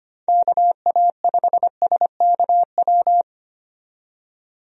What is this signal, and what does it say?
Morse code, heard as a keyed tone, about 25 wpm, 700 Hz. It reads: KA5SKW
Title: USA Call Signs - 6 Characters - 25wpm